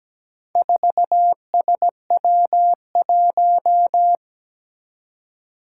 Morse code, heard as a keyed tone, about 17 wpm, 700 Hz. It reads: 4SW1
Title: Four Challenging Letter-Number Combinations 17wpm